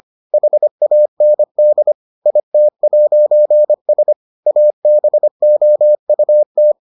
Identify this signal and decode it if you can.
Morse code, heard as a keyed tone, about 25 wpm, 600 Hz. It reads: HAND IT'S ABOUT